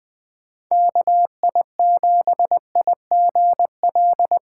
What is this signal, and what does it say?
Morse code, heard as a keyed tone, about 21 wpm, 700 Hz. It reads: KI7IGL
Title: USA Call Signs - 6 Characters - 2x Character Spacing - 20wpm